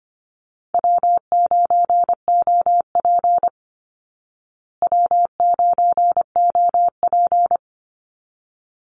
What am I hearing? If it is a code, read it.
Morse code, heard as a keyed tone, about 25 wpm, 700 Hz. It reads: W9OP W9OP
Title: Call Signs Repeated 3x - 25wpm